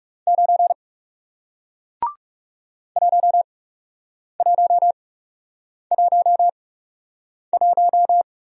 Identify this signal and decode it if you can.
Morse code, heard as a keyed tone, about 40 wpm, spaced as for 8 wpm, 700 Hz. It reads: 9 1111